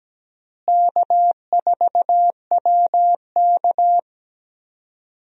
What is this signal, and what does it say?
Morse code, heard as a keyed tone, about 17 wpm, 700 Hz. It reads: K4WK
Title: USA Call Signs - 4 Characters - Repeated 2x - 17wpm